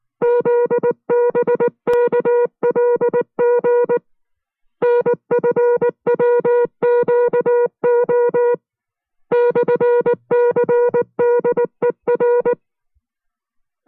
Morse code, heard as a keyed tone, about 19 wpm, 480 Hz. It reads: ZBKLG NFWQO /CDER